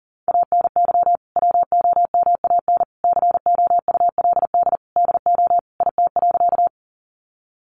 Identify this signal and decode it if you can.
Morse code, heard as a keyed tone, about 40 wpm, 700 Hz. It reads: ANY WOMAN COULD DO IT.
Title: Sentences from Top 1100 Words 40wpm